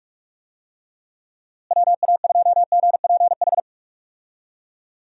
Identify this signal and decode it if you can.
Morse code, heard as a keyed tone, about 46 wpm, 700 Hz. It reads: WA2GPH
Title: USA Call Signs - 6 Characters - Repeated 3x - 45wpm